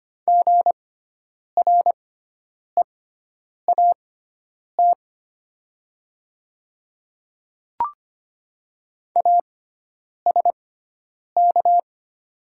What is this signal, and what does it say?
Morse code, heard as a keyed tone, about 25 wpm, spaced as for 9 wpm, 690 Hz. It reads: GREAT ASK
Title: 6x Character Spacing - Top 200 Words - 25wpm